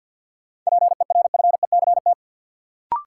Morse code, heard as a keyed tone, about 52 wpm, 700 Hz. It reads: PERFECT
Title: Sentences from Top 1000 Words - New Word Drill - 50wpm